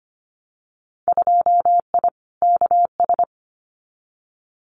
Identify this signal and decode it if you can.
Morse code, heard as a keyed tone, about 25 wpm, 700 Hz. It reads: 2I KS
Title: Winter Field Day Exchanges Encore 25wpm